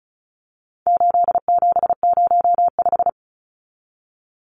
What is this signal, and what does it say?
Morse code, heard as a keyed tone, about 34 wpm, 700 Hz. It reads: 8705
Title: Numbers - 4-Digit 35wpm